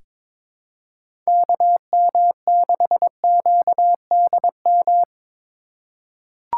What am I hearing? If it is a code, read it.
Morse code, heard as a keyed tone, about 22 wpm, 700 Hz. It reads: KM6QDM
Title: USA Call Signs - 6 Characters - 3x Character Spacing - 22wpm